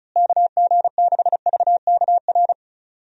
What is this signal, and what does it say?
Morse code, heard as a keyed tone, about 34 wpm, 700 Hz. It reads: KG6VKR
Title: USA Call Signs - 6 Characters - Repeated 3x - 35wpm